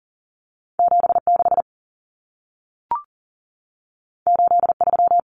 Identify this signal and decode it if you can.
Morse code, heard as a keyed tone, about 40 wpm, 700 Hz. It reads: 76 83
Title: Numbers - 2-Digit 40wpm